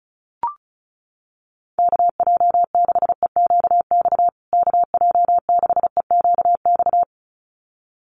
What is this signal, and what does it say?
Morse code, heard as a keyed tone, about 34 wpm, 700 Hz. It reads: KJ6EQX KJ6EQX
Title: USA Call Signs Repeated 2x 35wpm